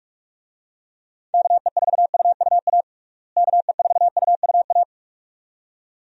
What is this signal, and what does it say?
Morse code, heard as a keyed tone, about 41 wpm, 700 Hz. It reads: KE4UUA KE4UUA
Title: USA Call Signs - 6 Characters - Repeated 2x - 45wpm